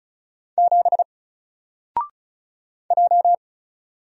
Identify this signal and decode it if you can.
Morse code, heard as a keyed tone, about 34 wpm, 700 Hz. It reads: 7 J
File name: Single-Letter-Number-35wpm